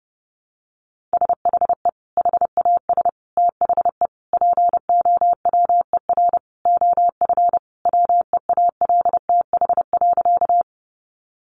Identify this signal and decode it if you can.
Morse code, heard as a keyed tone, about 30 wpm, 700 Hz. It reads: SHE HAS THE POWER OF WEALTH.
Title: Sentences from Top 1200 Words 30wpm